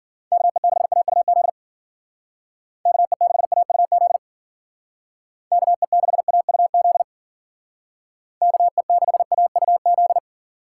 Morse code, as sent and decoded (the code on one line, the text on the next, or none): -.- . -.... .- ..- --.. / -.- . -.... .- ..- --.. / -.- . -.... .- ..- --.. / -.- . -.... .- ..- --..
KE6AUZ KE6AUZ KE6AUZ KE6AUZ